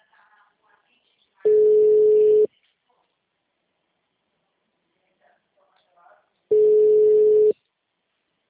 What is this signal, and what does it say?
Ringback tone (Europe, 425 Hz), with speech.